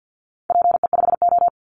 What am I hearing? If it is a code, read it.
Morse code, heard as a keyed tone, about 50 wpm, 700 Hz. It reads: PE5O
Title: Call Signs - 50wpm